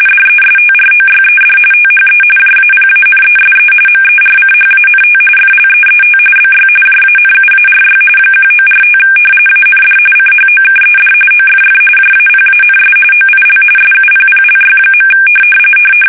RTTY, about 45.45 baud. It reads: DESTROYED THE JAPANESE MERCHANT FLEET BY JANUARY 1945 AND CHOKED OFF JAPANS OIL SUPPLY.
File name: fsk850